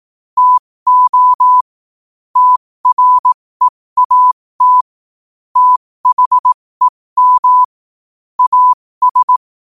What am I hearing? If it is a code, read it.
Morse code, heard as a keyed tone, about 18 wpm, spaced as for 15 wpm, 1000 Hz. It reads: TO TREAT THEM AS
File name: qotdfiles / 15wpm